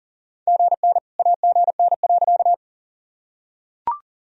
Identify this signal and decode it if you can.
Morse code, heard as a keyed tone, about 40 wpm, 700 Hz. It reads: GN AGN.
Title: Sentences from Top 300 Words and Abbreviations 40wpm